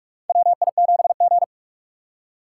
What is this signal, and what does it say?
Morse code, heard as a keyed tone, about 42 wpm, 690 Hz. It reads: WI7G
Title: USA Call Signs - 4 Characters - 3x Character Spacing - 45wpm